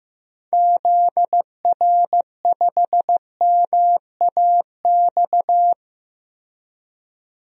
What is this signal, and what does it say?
Morse code, heard as a keyed tone, about 16 wpm, 700 Hz. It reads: ZR5MAX
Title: DXCC-100 Call Signs - 6 Characters - Repeated 3x - 15wpm